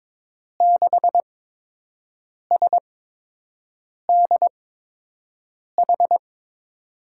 Morse code, heard as a keyed tone, about 22 wpm, spaced as for 6 wpm, 700 Hz. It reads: 6SDH
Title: USA Call Signs - 6 Characters - 8x Character Spacing - 22wpm